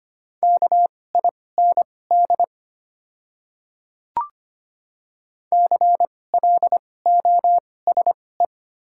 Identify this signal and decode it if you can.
Morse code, heard as a keyed tone, about 25 wpm, spaced as for 18 wpm, 690 Hz. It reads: KIND CLOSE
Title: 2x Character Spacing - Top 300 Words - 25wpm